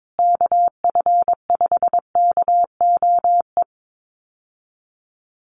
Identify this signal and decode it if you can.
Morse code, heard as a keyed tone, about 22 wpm, 700 Hz. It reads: KF5KOE
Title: USA Call Signs Repeated 3x 22wpm